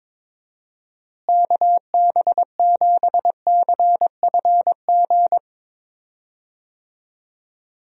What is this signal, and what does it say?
Morse code, heard as a keyed tone, about 21 wpm, 700 Hz. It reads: KB7CFG
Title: USA Call Signs - 6 Characters - Repeated 3x - 22wpm